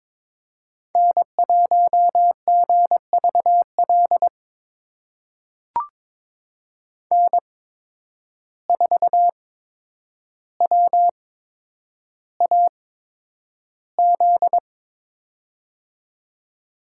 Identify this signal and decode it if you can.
Morse code, heard as a keyed tone, about 22 wpm, spaced as for 9 wpm, 700 Hz. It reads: N1GVL N4WAZ
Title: USA Call Signs - 5 Characters - 8x Character Spacing - 22wpm